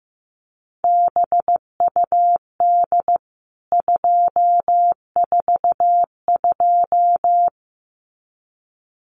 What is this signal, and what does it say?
Morse code, heard as a keyed tone, about 15 wpm, 700 Hz. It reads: BUD 242
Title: CWT Exchange 15wpm